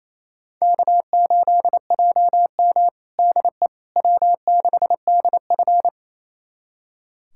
Morse code, heard as a keyed tone, about 27 wpm, 700 Hz. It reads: K8JM DE W6DF
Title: USA Callsign DE Callsign - 4 Characters - 28wpm